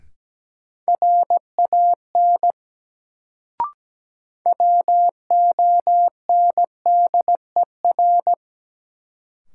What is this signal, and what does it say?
Morse code, heard as a keyed tone, about 17 wpm, 700 Hz. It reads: RAN WONDER
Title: Sentences from Top 500 Words - New Word Drill - 17wpm